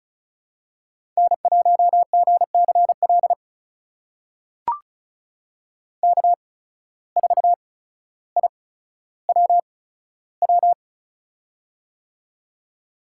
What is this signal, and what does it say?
Morse code, heard as a keyed tone, about 36 wpm, spaced as for 15 wpm, 700 Hz. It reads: N1GCL K4IWW
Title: USA Call Signs - 5 Characters - 8x Character Spacing - 35wpm